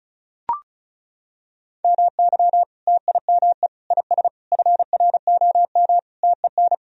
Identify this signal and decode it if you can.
Morse code, heard as a keyed tone, about 34 wpm, 700 Hz. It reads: MY TIME IS FROM TEN
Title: Sentences from Top 1000 Words - Review Part 2 - 35wpm